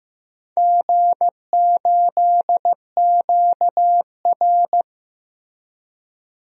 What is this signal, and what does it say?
Morse code, heard as a keyed tone, about 15 wpm, 700 Hz. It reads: G8QR
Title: DXCC-100 Call Signs - 4 Characters - Repeated 3x - 15wpm